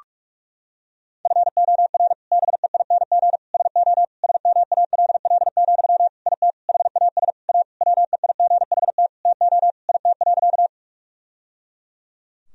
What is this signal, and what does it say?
Morse code, heard as a keyed tone, about 43 wpm, 700 Hz. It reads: FOR BEING SO SMALL, IT HAS A WEIGHT TO IT.
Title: Sentences from Top 1000 Words - Review Part 3 - 45wpm